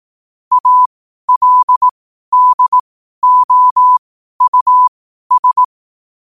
Morse code, heard as a keyed tone, about 18 wpm, spaced as for 12 wpm, 1000 Hz. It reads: ALDOUS
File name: qotdfiles / 12wpm